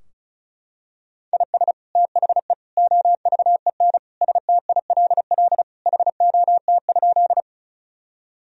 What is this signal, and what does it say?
Morse code, heard as a keyed tone, about 36 wpm, 700 Hz. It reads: IS THE OVEN STILL HOT?
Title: Sentences from Top 1000 Words - Review Part 4 - 35wpm